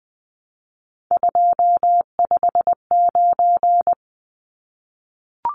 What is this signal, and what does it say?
Morse code, heard as a keyed tone, about 20 wpm, 700 Hz. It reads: 259
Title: Numbers - 3-Digit 20wpm